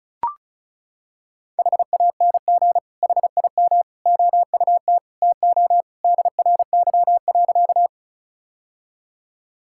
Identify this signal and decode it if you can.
Morse code, heard as a keyed tone, about 36 wpm, 700 Hz. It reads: HANG HIM OUT TO DRY.
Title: Sentences from Top 1600 Words 35wpm